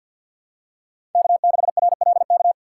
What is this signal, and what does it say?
Morse code, heard as a keyed tone, about 47 wpm, 700 Hz. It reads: K6RLK
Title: USA Call Signs - 5 Characters - 4x Character Spacing - 50wpm